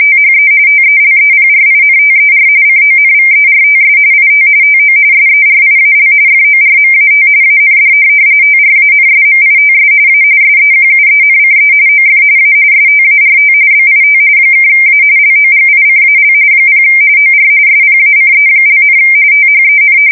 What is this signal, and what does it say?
RTTY, about 45.45 baud. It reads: THE TELEPRINTER CONVERTS IT TO SERIAL FORMAT AND TRANSMITS A SEQUENCE OF A START BIT (A LOGICAL 0 OR SPACE), THEN
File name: fsk170 / teletype-RTTY